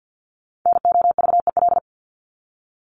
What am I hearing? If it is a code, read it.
Morse code, heard as a keyed tone, about 56 wpm, 700 Hz. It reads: NOVEL
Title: Top 1001-1250 Words 50wpm